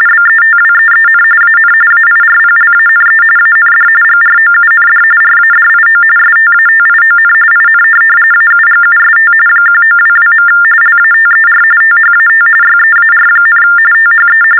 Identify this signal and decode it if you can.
TTY/TDD text telephone tones, about 45.45 baud. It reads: BETWEEN AN AIRPLANE AND GROUND RADIO STATION IN AUGUST 1922. LATER THAT YEAR, THE
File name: teletype-RTTY-45